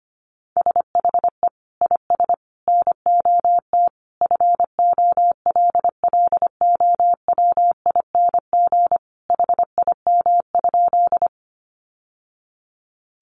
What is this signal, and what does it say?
Morse code, heard as a keyed tone, about 25 wpm, 700 Hz. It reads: SHE IS NOT FOLLOWING HIM?
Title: Sentences from Top 700 Words 25wpm